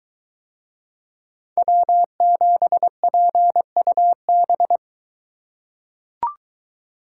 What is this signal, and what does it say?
Morse code, heard as a keyed tone, about 22 wpm, 700 Hz. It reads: W7PUB